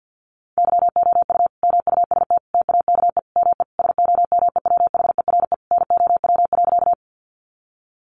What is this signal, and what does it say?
Morse code, heard as a keyed tone, about 50 wpm, 700 Hz. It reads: YOU MUST TAKE ME SOMEWHERE NOW.
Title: Sentences from Top 1400 Words 50wpm